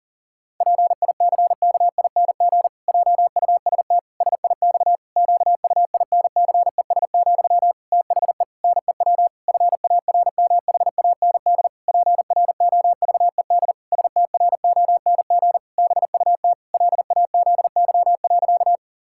Morse code, heard as a keyed tone, about 40 wpm, 700 Hz. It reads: PICKING JUST SIX QUINCES, THE NEW FARMHAND PROVED STRONG BUT LAZY.
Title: Pangram Challenge v2 - 40wpm